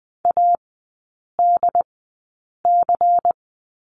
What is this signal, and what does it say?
Morse code, heard as a keyed tone, about 20 wpm, spaced as for 8 wpm, 700 Hz. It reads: ADC
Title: 093 Lesson 16e - Letter c - Three Characters 20wpm